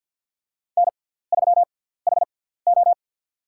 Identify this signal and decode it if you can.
Morse code, heard as a keyed tone, about 48 wpm, spaced as for 17 wpm, 700 Hz. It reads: N3HO
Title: USA Call Signs - 4 Characters - 6x Character Spacing - 50wpm